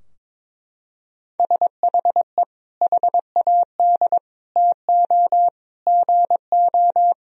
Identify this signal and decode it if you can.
Morse code, heard as a keyed tone, about 21 wpm, 700 Hz. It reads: SHE HAD TO GO
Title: Sentences from Top 1000 Words - Review Part 2 - 22wpm